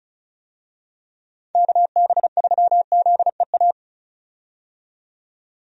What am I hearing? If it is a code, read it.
Morse code, heard as a keyed tone, about 36 wpm, 700 Hz. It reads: KB3ZEA
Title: USA Call Signs - 6 Characters - 35wpm